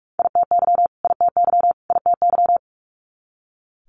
Morse code, heard as a keyed tone, about 46 wpm, 690 Hz. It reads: ITY ITY ITY
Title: Multi-letter Phonemes Repeated 3x 45wpm